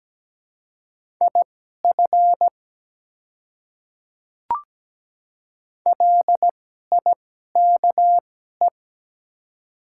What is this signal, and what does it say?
Morse code, heard as a keyed tone, about 17 wpm, spaced as for 12 wpm, 700 Hz. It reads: IF LIKE
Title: 2x Character Spacing - Top 100 Words - 17wpm